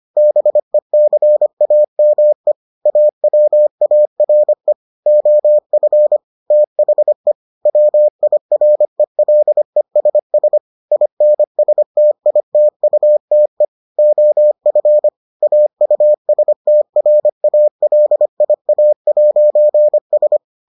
Morse code, heard as a keyed tone, about 25 wpm, 600 Hz. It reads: BECAME AWARE OF THE WIRELESS INSTITUTE OF AUSTRALIA'S